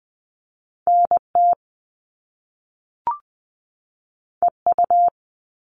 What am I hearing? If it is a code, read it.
Morse code, heard as a keyed tone, about 20 wpm, 700 Hz. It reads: NT EU
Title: Top Two Letter Combinations in Words 20wpm